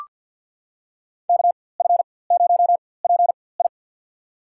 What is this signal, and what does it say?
Morse code, heard as a keyed tone, about 53 wpm, spaced as for 24 wpm, 700 Hz. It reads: KF0PI
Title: USA Call Signs - 5 Characters - 4x Character Spacing - 50wpm